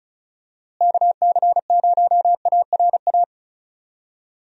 Morse code, heard as a keyed tone, about 34 wpm, 700 Hz. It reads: KC0ARA
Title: USA Call Signs - 6 Characters - Repeated 2x - 35wpm